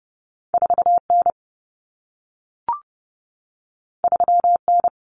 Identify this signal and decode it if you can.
Morse code, heard as a keyed tone, about 30 wpm, 700 Hz. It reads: 4N 3N
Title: Common Cut Numbers - 2-Digit 30wpm